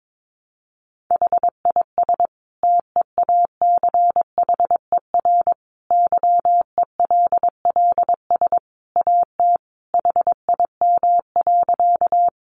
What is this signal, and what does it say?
Morse code, heard as a keyed tone, about 22 wpm, 700 Hz. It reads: HIS TEACHER YELLS AT HIM.
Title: Additional Easy Phrases 22wpm